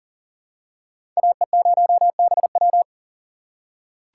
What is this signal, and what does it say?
Morse code, heard as a keyed tone, about 40 wpm, 700 Hz. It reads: AE0BW
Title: USA Call Signs - 5 Characters - 3x Character Spacing - 40wpm